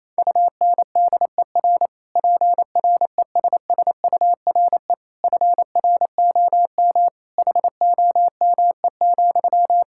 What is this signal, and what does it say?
Morse code, heard as a keyed tone, about 28 wpm, 700 Hz. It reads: UNDER PRESSURE FROM HOME,
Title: Sentences from Top 1000 Words - Review Part 6 - 28wpm